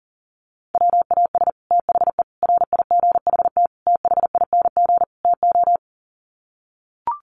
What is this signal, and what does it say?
Morse code, heard as a keyed tone, about 40 wpm, 700 Hz. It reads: WAS THE RIGHT THING TO
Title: Sets of 5 Words Encore2 40wpm